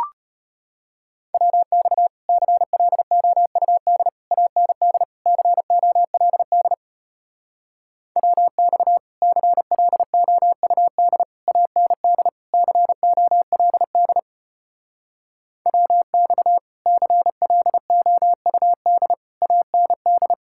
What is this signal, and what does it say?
Morse code, heard as a keyed tone, about 33 wpm, 700 Hz. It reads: WX CLOUD AND COLD WX CLOUD AND COLD WX CLOUD AND